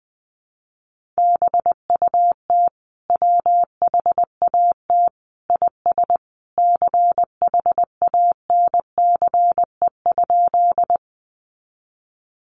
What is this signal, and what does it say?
Morse code, heard as a keyed tone, about 20 wpm, 700 Hz. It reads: BUT WHAT IS CHANCE?
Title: Sentences from Top 800 Words 20wpm